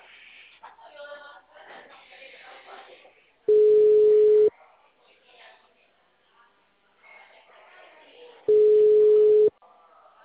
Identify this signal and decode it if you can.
Ringback tone (Europe, 425 Hz), with speech.